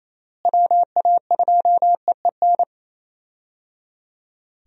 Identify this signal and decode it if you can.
Morse code, heard as a keyed tone, about 28 wpm, 700 Hz. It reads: WA2EEN
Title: USA Call Signs - 6 Characters - Repeated 3x - 28wpm